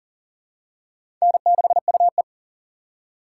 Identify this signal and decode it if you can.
Morse code, heard as a keyed tone, about 37 wpm, 700 Hz. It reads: N6UE
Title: USA Call Signs - 4 Characters - 6x Character Spacing - 40wpm